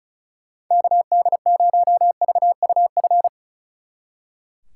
Morse code, heard as a keyed tone, about 46 wpm, 700 Hz. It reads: KD0VUF